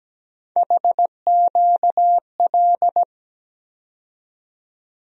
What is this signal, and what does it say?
Morse code, heard as a keyed tone, about 17 wpm, 700 Hz. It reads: HQL
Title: USA Call Sign Postfixes - 3 Letters 17wpm